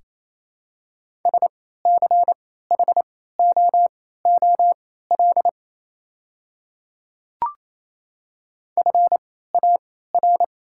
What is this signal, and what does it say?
Morse code, heard as a keyed tone, about 28 wpm, spaced as for 16 wpm, 690 Hz. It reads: SCHOOL FAR
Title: 3x Character Spacing - Top 300 Words - 28wpm